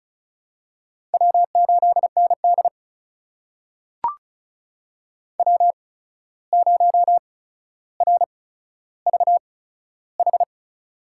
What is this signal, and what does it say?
Morse code, heard as a keyed tone, about 33 wpm, spaced as for 14 wpm, 700 Hz. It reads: W8ND W0RVH
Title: USA Call Signs - 8x Character Spacing - 35wpm